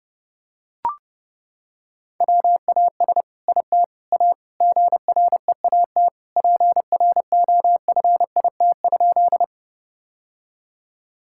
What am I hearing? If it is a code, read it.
Morse code, heard as a keyed tone, about 29 wpm, 700 Hz. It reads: WAS IT A GREAT PROFIT?
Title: Sentences from Top 1000 Words - Review Part 5 - 30wpm